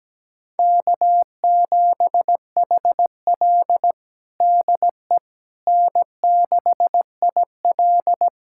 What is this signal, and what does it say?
Morse code, heard as a keyed tone, about 17 wpm, 700 Hz. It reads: K7HL DE N6IL
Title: USA Callsign DE Callsign - 4 Characters - 17wpm